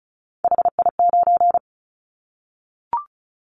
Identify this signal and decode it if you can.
Morse code, heard as a keyed tone, about 34 wpm, 700 Hz. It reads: HI9
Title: Callsign Prefixes for Top 100 DXCC - 35wpm